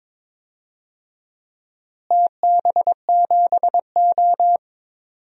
Morse code, heard as a keyed tone, about 22 wpm, 700 Hz. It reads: TB7O
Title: DXCC-100 Call Signs - 4 Characters - Repeated 3x - 22wpm